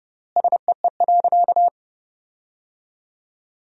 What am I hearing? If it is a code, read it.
Morse code, heard as a keyed tone, about 30 wpm, 700 Hz. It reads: SEE.
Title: 3x Word-Spacing - Sentences from Top 500 Words - 30wpm